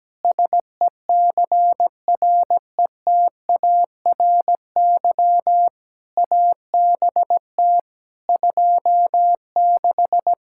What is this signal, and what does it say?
Morse code, heard as a keyed tone, about 17 wpm, 700 Hz. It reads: SECRETARY ABT 26
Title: K1USN SST QSO Practice 17wpm